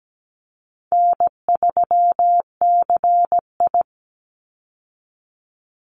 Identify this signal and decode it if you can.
Morse code, heard as a keyed tone, about 17 wpm, 700 Hz. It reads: N3CI
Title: Easy Call Signs Repeated 2x - 17wpm